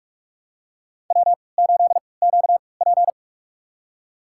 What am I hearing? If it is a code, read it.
Morse code, heard as a keyed tone, about 46 wpm, spaced as for 25 wpm, 700 Hz. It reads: W8QP
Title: USA Call Signs - 4 Characters - 3x Character Spacing - 45wpm